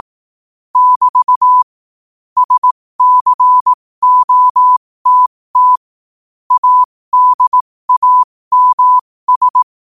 Morse code, heard as a keyed tone, about 18 wpm, spaced as for 15 wpm, 1000 Hz. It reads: <BT> SCOTT ADAMS
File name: qotdfiles / 15wpm